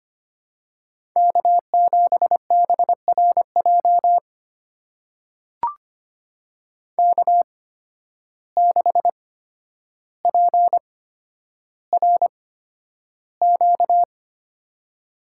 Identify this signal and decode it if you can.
Morse code, heard as a keyed tone, about 25 wpm, spaced as for 11 wpm, 700 Hz. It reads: K7BRJ K6PRQ
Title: USA Call Signs - 5 Characters - 8x Character Spacing - 25wpm